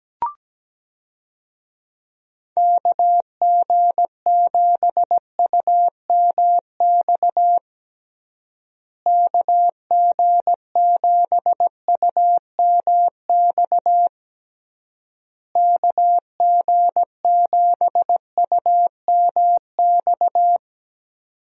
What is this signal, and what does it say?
Morse code, heard as a keyed tone, about 17 wpm, 700 Hz. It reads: KG7UMX KG7UMX KG7UMX
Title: USA Call Signs - 6 Characters - Repeated 3x - 17wpm